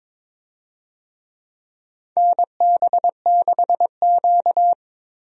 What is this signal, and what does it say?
Morse code, heard as a keyed tone, about 21 wpm, 700 Hz. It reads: NB6Q
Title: USA Call Signs - 4 Characters - 22wpm